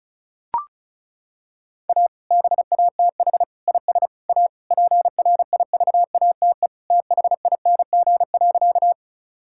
Morse code, heard as a keyed tone, about 34 wpm, 700 Hz. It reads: A BATH IS A PRIVATE THING.
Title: Sentences from Top 1600 Words 35wpm